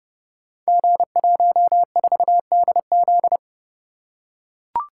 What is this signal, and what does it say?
Morse code, heard as a keyed tone, about 30 wpm, 700 Hz. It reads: G14DZ
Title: Code Groups - 5 Letter:Numbers - 30wpm